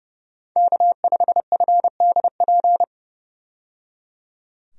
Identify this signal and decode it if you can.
Morse code, heard as a keyed tone, about 30 wpm, 700 Hz. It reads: K5FDP
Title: USA Call Signs - 5 Characters - 30wpm